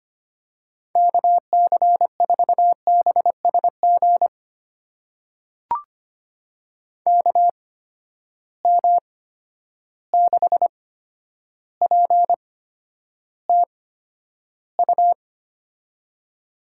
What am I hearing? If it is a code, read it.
Morse code, heard as a keyed tone, about 25 wpm, spaced as for 11 wpm, 700 Hz. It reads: KC4BSG KM6PTU
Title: USA Call Signs - 6 Characters - 8x Character Spacing - 25wpm